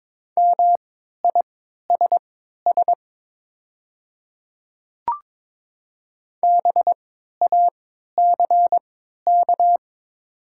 Morse code, heard as a keyed tone, about 23 wpm, spaced as for 13 wpm, 690 Hz. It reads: MISS BACK
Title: 3x Character Spacing - Top 500 Words - 22wpm